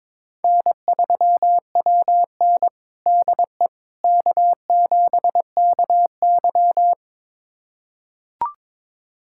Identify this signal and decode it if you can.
Morse code, heard as a keyed tone, about 22 wpm, 700 Hz. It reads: N3WN DE K7KY
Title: USA Callsign DE Callsign - 4 Characters - 22wpm